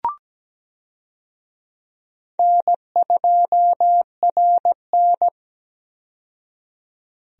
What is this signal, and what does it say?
Morse code, heard as a keyed tone, about 17 wpm, 700 Hz. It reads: N2RN
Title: USA Call Signs - 4 Characters - 17wpm